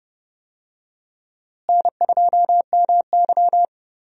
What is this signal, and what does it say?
Morse code, heard as a keyed tone, about 30 wpm, 700 Hz. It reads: N2MY
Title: USA Call Signs - 4 Characters - Repeated 2x - 30wpm